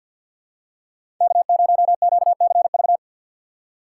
Morse code, heard as a keyed tone, about 47 wpm, 690 Hz. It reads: K0QCV
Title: USA Call Signs - 5 Characters - 6x Character Spacing - 50wpm